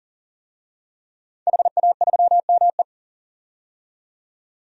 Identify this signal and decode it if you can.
Morse code, heard as a keyed tone, about 40 wpm, 700 Hz. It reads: HA3ME
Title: DXCC-100 Call Signs - 5 Characters - 40wpm